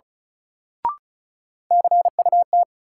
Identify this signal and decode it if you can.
Morse code, heard as a keyed tone, about 36 wpm, 700 Hz. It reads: CUT
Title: Sentences from Top 300 Words - New Word Drill - 35wpm